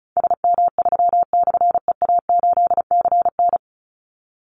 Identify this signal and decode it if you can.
Morse code, heard as a keyed tone, about 34 wpm, 700 Hz. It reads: SM3/EA8CN
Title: Call Signs - 35wpm